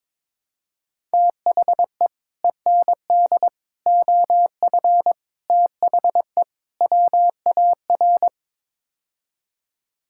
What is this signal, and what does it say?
Morse code, heard as a keyed tone, about 22 wpm, 700 Hz. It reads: THE END OF THE WAR
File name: practice / 22wpm / Build A Phrase - 5-Word Phrases - 22wpm